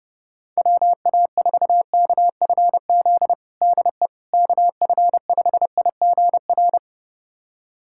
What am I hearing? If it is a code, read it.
Morse code, heard as a keyed tone, about 30 wpm, 700 Hz. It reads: WA4KFZ DE KF5IGR
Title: USA Callsign DE Callsign - 6 Characters - 30wpm